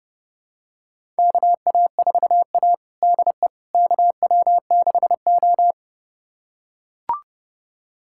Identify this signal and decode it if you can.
Morse code, heard as a keyed tone, about 30 wpm, 700 Hz. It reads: KA4A DE KW6O
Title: USA Callsign DE Callsign - 4 Characters - 30wpm